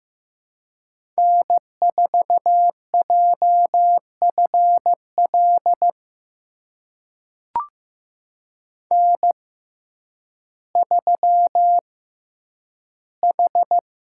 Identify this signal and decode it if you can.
Morse code, heard as a keyed tone, about 15 wpm, spaced as for 9 wpm, 700 Hz. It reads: N4JFL N3H
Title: USA Call Signs - 5 Characters - 6x Character Spacing - 15wpm